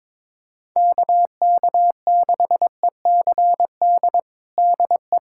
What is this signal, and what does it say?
Morse code, heard as a keyed tone, about 22 wpm, 700 Hz. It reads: KK6ECD DE
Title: USA Callsign DE Callsign - 6 Characters - 22wpm